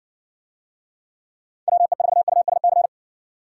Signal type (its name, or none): Morse code